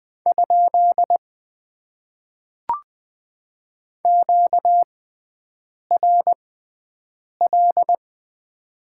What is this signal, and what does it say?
Morse code, heard as a keyed tone, about 20 wpm, spaced as for 7 wpm, 700 Hz. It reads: ? QRL
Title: Top 90 QSO Elements - 6x Character Spacing - 20wpm